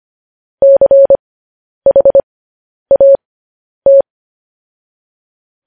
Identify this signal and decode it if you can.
Morse code, heard as a keyed tone, about 25 wpm, spaced as for 10 wpm, 550 Hz. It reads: CHAT
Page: news-headlines-in-morse-code-at-10-wpm